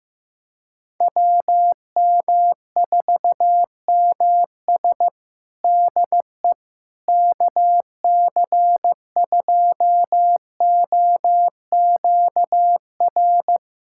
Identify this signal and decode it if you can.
Morse code, heard as a keyed tone, about 15 wpm, 700 Hz. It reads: WM4MS DE KC2OQR
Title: USA Callsign DE Callsign 15wpm